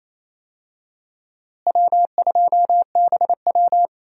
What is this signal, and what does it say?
Morse code, heard as a keyed tone, about 28 wpm, 700 Hz. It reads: W2BW
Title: USA Call Signs - 4 Characters - Repeated 3x - 28wpm